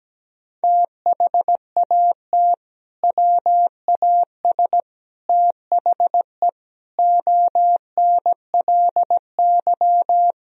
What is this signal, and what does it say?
Morse code, heard as a keyed tone, about 17 wpm, 700 Hz. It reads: THAT WAS THE ONLY
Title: Sentences from Top 1000 Words - Review Part 1 - 17wpm